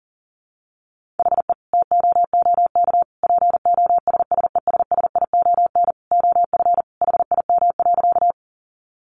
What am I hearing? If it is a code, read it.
Morse code, heard as a keyed tone, about 37 wpm, 700 Hz. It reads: HE TOOK POSSESSION OF HIM.
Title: Sentences from Top 1400 Words 40wpm